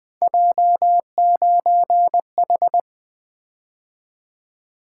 Morse code, heard as a keyed tone, about 20 wpm, 700 Hz. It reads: J9H
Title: Three Letter-Number Combinations - 20wpm